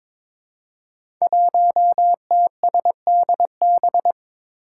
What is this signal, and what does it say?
Morse code, heard as a keyed tone, about 22 wpm, 700 Hz. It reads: 1TSDB
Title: Five Challenging Letter-Number Combinations 22wpm